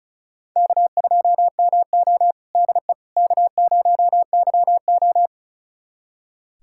Morse code, heard as a keyed tone, about 34 wpm, 700 Hz. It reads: K2MO DE K0YO
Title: USA Callsign DE Callsign - 4 Characters - 35wpm